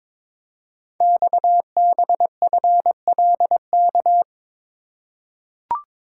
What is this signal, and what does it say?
Morse code, heard as a keyed tone, about 23 wpm, 700 Hz. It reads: XBFLK
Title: Code Groups - 5 Letters - 22wpm